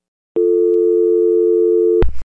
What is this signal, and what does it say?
Dial tone (North America), 350 + 440 Hz.